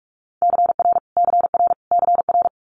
Morse code, heard as a keyed tone, about 46 wpm, 700 Hz. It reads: CR CR CR
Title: Multi-letter Phonemes Repeated 3x 45wpm